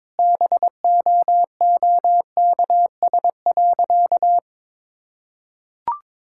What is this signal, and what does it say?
Morse code, heard as a keyed tone, about 22 wpm, 700 Hz. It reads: BOOKS.
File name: practice / 22wpm / Sentences with 8-Word Phrases - 22wpm